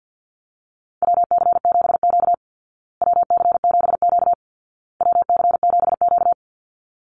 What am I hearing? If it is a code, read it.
Morse code, heard as a keyed tone, about 50 wpm, 700 Hz. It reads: WC7Q WC7Q WC7Q
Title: Call Signs Repeated 2x - 50wpm